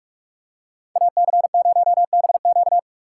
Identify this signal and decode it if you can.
Morse code, heard as a keyed tone, about 44 wpm, 700 Hz. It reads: AC0BQ